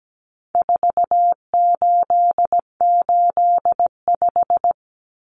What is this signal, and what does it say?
Morse code, heard as a keyed tone, about 17 wpm, 700 Hz. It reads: 4885
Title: Common Cut Numbers - 4-Digit 17wpm